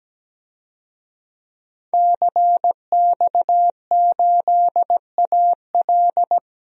Morse code, heard as a keyed tone, about 17 wpm, 700 Hz. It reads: CX8AL
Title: DXCC-100 Call Signs - 5 Characters - 17wpm